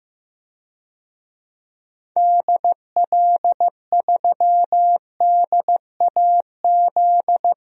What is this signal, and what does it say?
Morse code, heard as a keyed tone, about 15 wpm, 700 Hz. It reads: DL3DAZ
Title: DXCC-100 Call Signs - 6 Characters - 15wpm